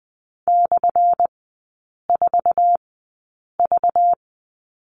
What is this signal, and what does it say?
Morse code, heard as a keyed tone, about 20 wpm, spaced as for 8 wpm, 690 Hz. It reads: /4V
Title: 205 Lesson 33e - Letter j - Three Characters 20wpm